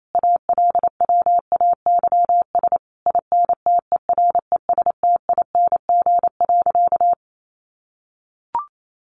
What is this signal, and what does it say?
Morse code, heard as a keyed tone, about 28 wpm, 700 Hz. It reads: ALWAYS INTERESTING.
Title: Sentences from Top 1200 Words 28wpm